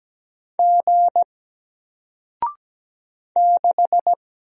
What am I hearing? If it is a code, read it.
Morse code, heard as a keyed tone, about 17 wpm, 700 Hz. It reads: G 6
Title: Single Character (Letter:Number:Punctuation:Prosign) Alt 17wpm